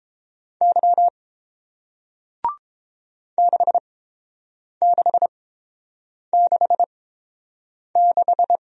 Morse code, heard as a keyed tone, about 30 wpm, spaced as for 7 wpm, 700 Hz. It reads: Y 6666